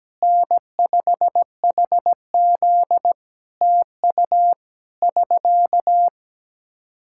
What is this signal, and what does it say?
Morse code, heard as a keyed tone, about 17 wpm, 700 Hz. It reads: N5HZ TU <SK>
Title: Practice QSO - Exchange 4 - Ending - 17wpm